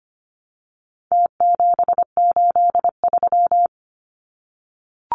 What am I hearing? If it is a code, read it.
Morse code, heard as a keyed tone, about 25 wpm, 700 Hz. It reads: T783
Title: Numbers - Contest Style - 4-Digit 25wpm